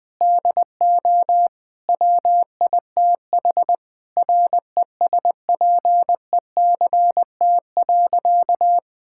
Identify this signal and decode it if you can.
Morse code, heard as a keyed tone, about 20 wpm, 700 Hz. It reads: DO WITH RESPECT.
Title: Sentences with 4-Word Phrases - 20wpm